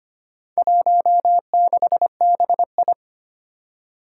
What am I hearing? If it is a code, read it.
Morse code, heard as a keyed tone, about 25 wpm, 700 Hz. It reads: 16BI
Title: Four Challenging Letter-Number Combinations 25wpm